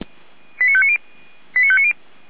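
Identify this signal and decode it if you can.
Five-tone selective call (ZVEI), heard as quick runs of five tones, tones of about 70 ms. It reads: 89590 89590
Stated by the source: ZVEI five-tone selective call received on SDR, from the Feuerwehr Orlamünde audio server.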